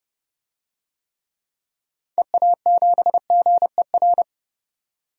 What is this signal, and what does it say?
Morse code, heard as a keyed tone, about 30 wpm, 700 Hz. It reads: EA7GER
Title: DXCC-100 Call Signs - 6 Characters - 30wpm